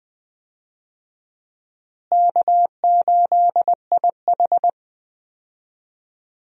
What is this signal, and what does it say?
Morse code, heard as a keyed tone, about 21 wpm, 700 Hz. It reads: K8IH
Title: USA Call Signs - 4 Characters - Repeated 2x - 20wpm